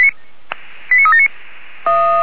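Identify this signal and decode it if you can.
Five-tone selective call (ZVEI), heard as quick runs of five tones, tones of about 70 ms. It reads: 89379
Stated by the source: ZVEI five-tone selective call 89376 received by SDR.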